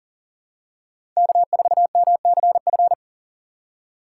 Morse code, heard as a keyed tone, about 40 wpm, 700 Hz. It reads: K4MCF
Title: USA Call Signs - 5 Characters - 2x Character Spacing - 40wpm